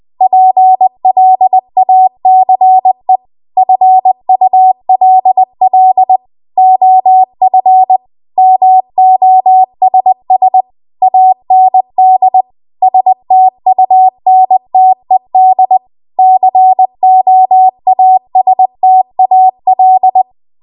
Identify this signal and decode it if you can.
Morse code, heard as a keyed tone, about 20 wpm, 750 Hz. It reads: PLACE FULL OF MOSS AND STUNTED COASTAL
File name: Morse / 20WPM